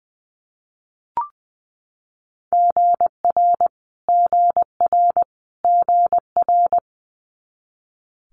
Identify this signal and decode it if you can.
Morse code, heard as a keyed tone, about 20 wpm, 700 Hz. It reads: GR GR GR
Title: Multi-letter Phonemes Repeated 3x 20wpm